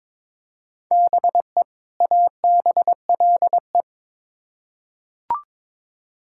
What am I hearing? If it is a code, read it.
Morse code, heard as a keyed tone, about 22 wpm, 700 Hz. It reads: BE ABLE
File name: Build A Phrase - 2-Word Phrases - 22wpm